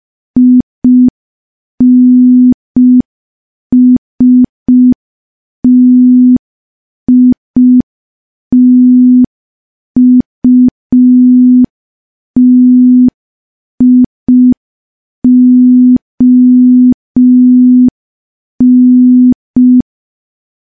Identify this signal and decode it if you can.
Morse code, heard as a keyed tone, about 5 wpm, 250 Hz. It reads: INSTITUTION
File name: morse / morse-training-05WPM